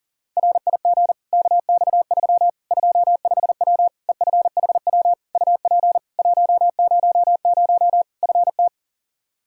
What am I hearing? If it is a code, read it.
Morse code, heard as a keyed tone, about 37 wpm, 700 Hz. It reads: RIG KX3 25W EFHW UP 100 FT
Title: Practice QSO - Exchange 2 - Info - 40wpm